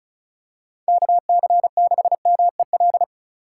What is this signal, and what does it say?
Morse code, heard as a keyed tone, about 34 wpm, 690 Hz. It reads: KC6MEL
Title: USA Call Signs - 4x Character Spacing - 35wpm